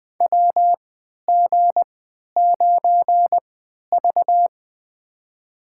Morse code, heard as a keyed tone, about 20 wpm, spaced as for 11 wpm, 700 Hz. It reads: WG9V
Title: USA Call Signs - 3x Character Spacing - 20wpm